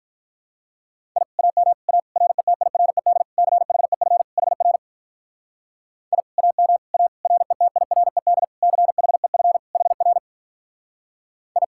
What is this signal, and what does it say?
Morse code, heard as a keyed tone, about 43 wpm, 700 Hz. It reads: I AM A RETIRED CHEF HR I AM A RETIRED CHEF HR I
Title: Speed-Racing Common QSO Phrases - 35wpm